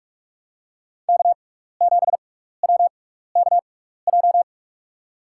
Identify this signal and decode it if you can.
Morse code, heard as a keyed tone, about 44 wpm, spaced as for 15 wpm, 690 Hz. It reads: K7WKJ
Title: USA Call Signs - 5 Characters - 6x Character Spacing - 45wpm